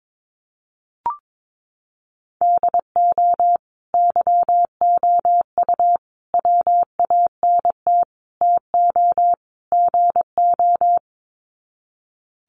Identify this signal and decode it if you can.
Morse code, heard as a keyed tone, about 22 wpm, 700 Hz. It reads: DO YOU WANT TO GO
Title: Sets of 5 Words Encore2 22wpm